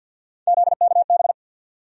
Morse code, heard as a keyed tone, about 53 wpm, 700 Hz. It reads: ZKB
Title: USA Call Sign Postfixes - 3 Letters 50wpm